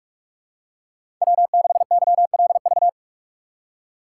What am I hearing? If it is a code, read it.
Morse code, heard as a keyed tone, about 41 wpm, 700 Hz. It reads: W6YLV
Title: USA Call Signs - 5 Characters - 2x Character Spacing - 45wpm